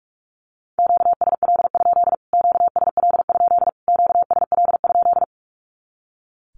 Morse code, heard as a keyed tone, about 44 wpm, 700 Hz. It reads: QSL? QSL? QSL?
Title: Q-Codes Repeated 3x 45wpm